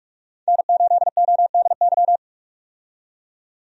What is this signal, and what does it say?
Morse code, heard as a keyed tone, about 46 wpm, 700 Hz. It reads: N8ODY
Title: USA Call Signs - 5 Characters - 45wpm